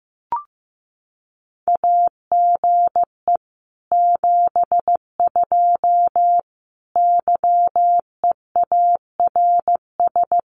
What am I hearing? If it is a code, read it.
Morse code, heard as a keyed tone, about 15 wpm, 700 Hz. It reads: AGE 72 YEARS
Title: Common QSO Phrases Encore 15wpm